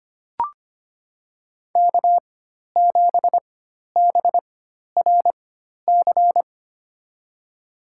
Morse code, heard as a keyed tone, about 25 wpm, spaced as for 12 wpm, 700 Hz. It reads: K7BRC
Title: USA Call Signs - 5 Characters - 4x Character Spacing - 25wpm